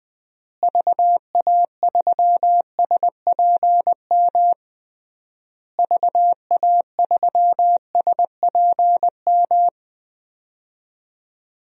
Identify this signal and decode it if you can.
Morse code, heard as a keyed tone, about 20 wpm, 700 Hz. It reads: VA3SPM VA3SPM
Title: DXCC-100 Call Signs - 6 Characters - Repeated 3x - 20wpm